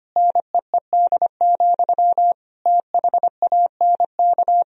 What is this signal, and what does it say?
Morse code, heard as a keyed tone, about 25 wpm, 700 Hz. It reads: NEED, THANK
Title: Sentences from Top 1000 Words - Review Part 1 - 25wpm